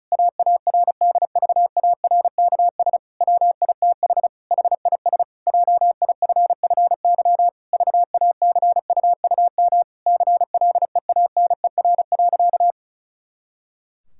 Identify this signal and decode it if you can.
Morse code, heard as a keyed tone, about 36 wpm, 700 Hz. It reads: AARDVARKS WITH HIS JIFFY VACUUM CLEANER.
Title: Pangram Challenge v2 - 35wpm